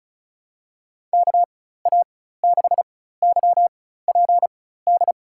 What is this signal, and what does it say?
Morse code, heard as a keyed tone, about 34 wpm, spaced as for 16 wpm, 700 Hz. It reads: KA6YPD
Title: USA Call Signs - 6 Characters - 4x Character Spacing - 35wpm